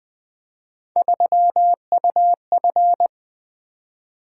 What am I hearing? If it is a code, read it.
Morse code, heard as a keyed tone, about 20 wpm, 700 Hz. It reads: 3UF
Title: Three Letter-Number Combinations - 20wpm